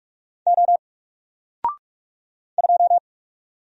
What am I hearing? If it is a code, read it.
Morse code, heard as a keyed tone, about 44 wpm, 700 Hz. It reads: O 2
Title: Single Letter-Number 45wpm